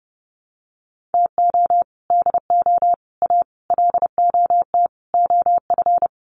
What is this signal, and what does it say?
Morse code, heard as a keyed tone, about 30 wpm, 700 Hz. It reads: TO DO A LOT OF
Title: Sets of 5 Words Encore 30wpm